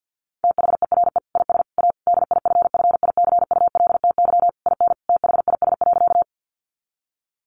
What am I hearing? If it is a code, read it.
Morse code, heard as a keyed tone, about 50 wpm, 700 Hz. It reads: THERE IS A DIFFICULTY IN THIS.
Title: Sentences from Top 1500 Words 50wpm